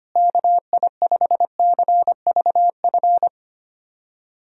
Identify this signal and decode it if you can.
Morse code, heard as a keyed tone, about 25 wpm, 700 Hz. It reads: KI5CVF
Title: USA Call Signs - 6 Characters - Repeated 2x - 25wpm